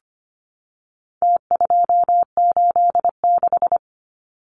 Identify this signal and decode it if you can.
Morse code, heard as a keyed tone, about 25 wpm, 700 Hz. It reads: T286
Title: Numbers - Contest Style - 4-Digit 25wpm